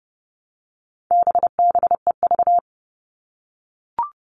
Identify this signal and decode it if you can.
Morse code, heard as a keyed tone, about 30 wpm, 700 Hz. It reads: BBEV
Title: All Cut Numbers - 4-Digit 30wpm